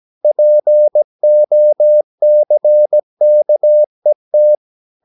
Morse code, heard as a keyed tone, about 17 wpm, 600 Hz. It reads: POCKET
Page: Romanen Never af Ken Follet på engelsk.